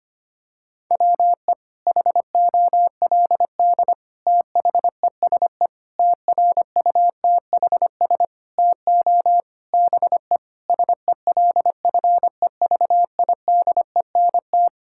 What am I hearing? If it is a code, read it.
Morse code, heard as a keyed tone, about 25 wpm, 700 Hz. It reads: WE HOLD THESE TRUTHS TO BE SELFEVIDENT
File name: Sets of 9 Words - 25wpm